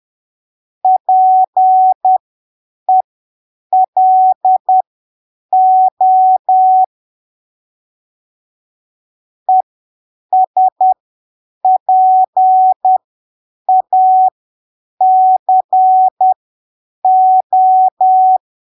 Morse code, heard as a keyed tone, about 10 wpm, spaced as for 7 wpm, 750 Hz. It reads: PELO ESPACO
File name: arquivos / texto / texto 21 10ppm